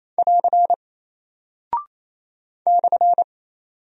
Morse code, heard as a keyed tone, about 28 wpm, 700 Hz. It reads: <AR> /
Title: Single Character (Letter:Number:Punctuation:Prosign) Alt 28wpm